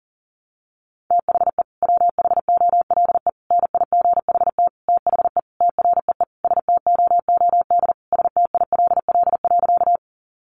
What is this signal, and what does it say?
Morse code, heard as a keyed tone, about 40 wpm, 700 Hz. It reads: THE WHOLE NIGHT THE TREE STOOD STILL.
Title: Sentences from Top 500 Words 40wpm